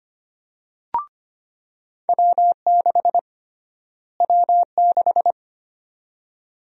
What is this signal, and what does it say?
Morse code, heard as a keyed tone, about 25 wpm, 700 Hz. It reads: W6 W6
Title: USA Call Sign Prefixes - Repeated 2x - 25wpm